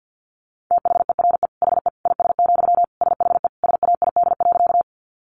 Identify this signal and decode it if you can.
Morse code, heard as a keyed tone, about 50 wpm, 700 Hz. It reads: THERE HE IS, SHE SAID.
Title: Sentences from Top 100 Words 50wpm